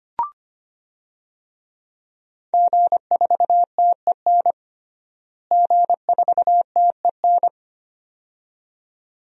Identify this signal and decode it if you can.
Morse code, heard as a keyed tone, about 25 wpm, 700 Hz. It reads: G4TEN G4TEN
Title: DXCC-100 Call Signs - 5 Characters - Repeated 2x - 25wpm